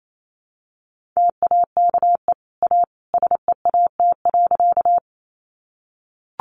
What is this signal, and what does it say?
Morse code, heard as a keyed tone, about 28 wpm, 700 Hz. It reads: TAKE A SEAT.
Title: Sentences from Top 1500 Words 28wpm